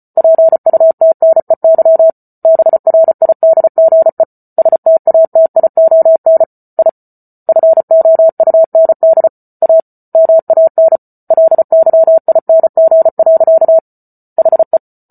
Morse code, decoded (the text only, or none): PUTNEY BRIDGE STATION I FOUND A MAN LYING. HE